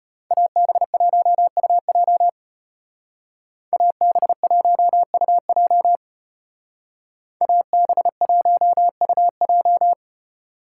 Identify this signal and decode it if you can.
Morse code, heard as a keyed tone, about 34 wpm, 700 Hz. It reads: AB1UJ AB1UJ AB1UJ